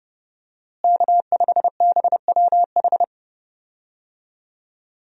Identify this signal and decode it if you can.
Morse code, heard as a keyed tone, about 31 wpm, 700 Hz. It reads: K5BWH
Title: USA Call Signs - 5 Characters - Repeated 2x - 30wpm